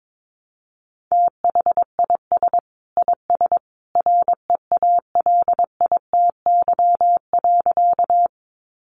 Morse code, heard as a keyed tone, about 22 wpm, 700 Hz. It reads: THIS IS REALITY.
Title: Sentences from Top 1200 Words 22wpm